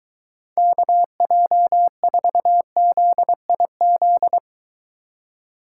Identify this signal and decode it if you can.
Morse code, heard as a keyed tone, about 23 wpm, 700 Hz. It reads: KJ4ZIZ